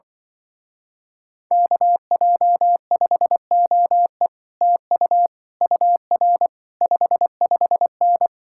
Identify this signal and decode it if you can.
Morse code, heard as a keyed tone, about 24 wpm, 700 Hz. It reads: KJ5OE TU UR 55N
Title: Speed-Racing Common QSO Phrases - 20wpm